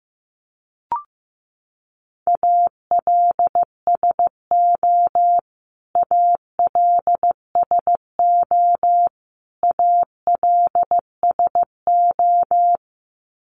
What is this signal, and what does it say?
Morse code, heard as a keyed tone, about 15 wpm, 700 Hz. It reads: ALSO ALSO ALSO
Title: Top 500 Words Repeated 15wpm